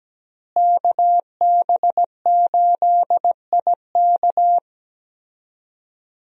Morse code, heard as a keyed tone, about 17 wpm, 700 Hz. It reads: KB8IK
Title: USA Call Signs - 5 Characters - Repeated 3x - 17wpm